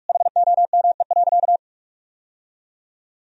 Morse code, heard as a keyed tone, about 46 wpm, 700 Hz. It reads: HOME.
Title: Sentences from Top 1000 Words - Review Part 3 - 45wpm